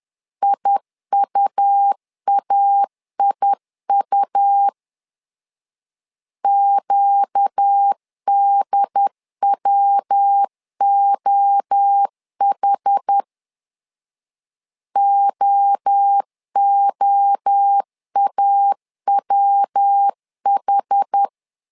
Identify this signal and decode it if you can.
Morse code, heard as a keyed tone, about 10 wpm, 800 Hz. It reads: IUAIU QDWOH OOAWH
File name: LEZIONI / 4 Q 10wpm 2210